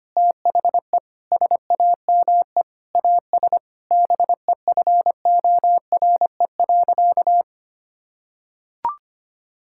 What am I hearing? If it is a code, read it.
Morse code, heard as a keyed tone, about 25 wpm, 700 Hz. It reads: THE SAME AS BEFORE.
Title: Sentences from Top 1000 Words - Review Part 6 - 25wpm